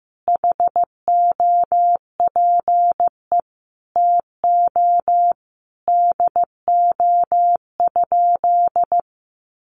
Morse code, heard as a keyed tone, about 15 wpm, 700 Hz. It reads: HOPE TO DO?
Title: Sentences from Top 1000 Words 15wpm